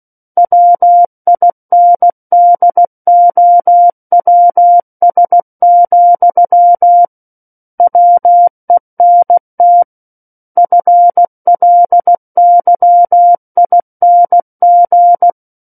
Morse code, heard as a keyed tone, about 16 wpm, 700 Hz. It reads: WINDOWS, WENT FLYING
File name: Book One / 07-Chapter 7 - 16 WPM